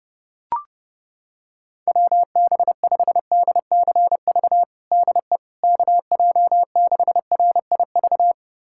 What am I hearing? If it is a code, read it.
Morse code, heard as a keyed tone, about 32 wpm, 700 Hz. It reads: WB5DCV DE KJ6RIV
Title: USA Callsign DE Callsign - 6 Characters - 30wpm